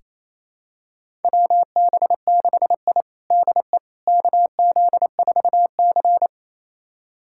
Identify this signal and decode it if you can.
Morse code, heard as a keyed tone, about 28 wpm, 700 Hz. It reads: WB6I DE KZ4C
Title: USA Callsign DE Callsign - 4 Characters - 28wpm